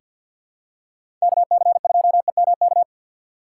Morse code, heard as a keyed tone, about 47 wpm, 700 Hz. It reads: KC2EMK
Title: USA Call Signs - 3x Character Spacing - 50wpm